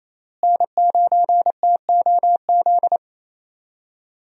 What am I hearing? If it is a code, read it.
Morse code, heard as a keyed tone, about 28 wpm, 700 Hz. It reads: N9TOZ
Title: USA Call Signs - 5 Characters - Repeated 3x - 28wpm